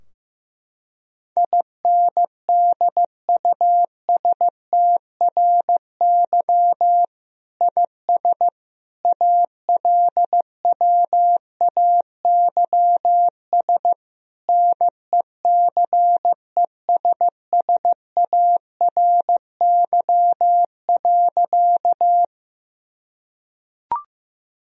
Morse code, heard as a keyed tone, about 15 wpm, 700 Hz. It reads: INDUSTRY IS ALWAYS NECESSARY.
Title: Sentences from Top 1000 Words - Review Part 4 - 15wpm